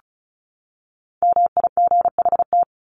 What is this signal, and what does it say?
Morse code, heard as a keyed tone, about 34 wpm, 700 Hz. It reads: MIGHT
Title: Top 300 Words 35wpm